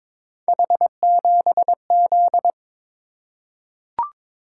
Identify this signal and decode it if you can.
Morse code, heard as a keyed tone, about 22 wpm, 700 Hz. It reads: H7Z
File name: practice / 22wpm / Three Letter-Number Combinations - 22wpm